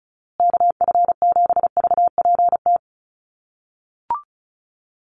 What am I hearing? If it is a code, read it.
Morse code, heard as a keyed tone, about 34 wpm, 700 Hz. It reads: KF7VPT
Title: USA Call Signs Repeated 3x 35wpm